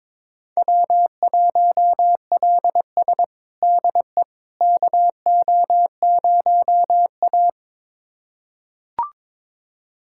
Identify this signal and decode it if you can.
Morse code, heard as a keyed tone, about 22 wpm, 700 Hz. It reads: W1LS DE KO0A
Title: USA Callsign DE Callsign - 4 Characters - 22wpm